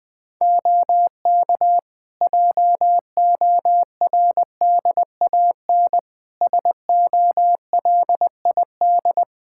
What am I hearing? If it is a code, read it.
Morse code, heard as a keyed tone, about 20 wpm, 700 Hz. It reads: OK JORDAN SOLID
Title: Practice QSO - Exchange 3 - More Info - 20wpm